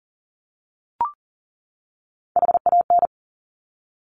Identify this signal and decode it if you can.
Morse code, heard as a keyed tone, about 42 wpm, 690 Hz. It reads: HAN
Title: Top Three Letter Combinations in Words 40wpm